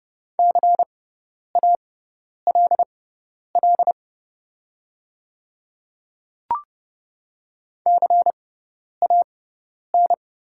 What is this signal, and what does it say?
Morse code, heard as a keyed tone, about 30 wpm, spaced as for 10 wpm, 700 Hz. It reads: CALL CAN
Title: 6x Character Spacing - Top 100 Words - 30wpm